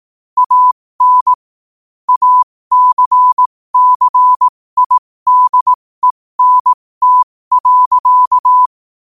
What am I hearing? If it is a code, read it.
Morse code, heard as a keyed tone, about 18 wpm, spaced as for 15 wpm, 1000 Hz. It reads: AN ACCIDENT.
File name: qotdfiles / 15wpm